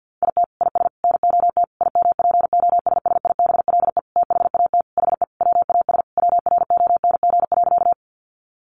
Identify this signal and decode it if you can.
Morse code, heard as a keyed tone, about 50 wpm, 700 Hz. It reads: IT IS NOT IMPOSSIBLE THAT HE WAS WRONG.
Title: Sentences from Top 1100 Words 50wpm